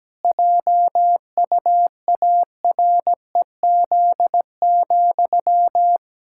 Morse code, heard as a keyed tone, about 17 wpm, 700 Hz. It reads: JUAREZ,
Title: Pangram Challenge v2 - 17wpm